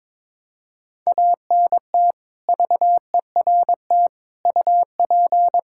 Morse code, heard as a keyed tone, about 22 wpm, 700 Hz. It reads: ANT VERT UP
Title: Practice QSO - Exchange 2 - Info - 22wpm